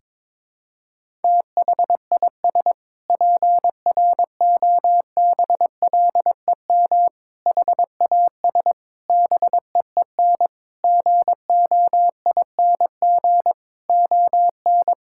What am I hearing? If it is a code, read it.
Morse code, heard as a keyed tone, about 22 wpm, 700 Hz. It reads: THIS PROBLEM HAS BEEN GOING ON
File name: practice / 22wpm / Sentences with 8-Word Phrases - 22wpm